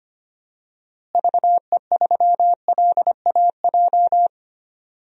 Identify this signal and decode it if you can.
Morse code, heard as a keyed tone, about 25 wpm, 700 Hz. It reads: VE3LAJ
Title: DXCC-100 Call Signs - 6 Characters - Repeated 2x - 25wpm